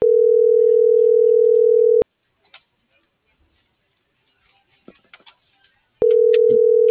Ringback tone (North America), 440 + 480 Hz.